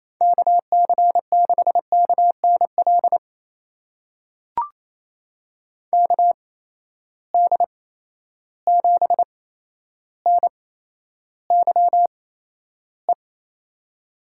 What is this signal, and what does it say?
Morse code, heard as a keyed tone, about 28 wpm, spaced as for 12 wpm, 700 Hz. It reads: KC6KNL KD7NYE
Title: USA Call Signs - 6 Characters - 8x Character Spacing - 28wpm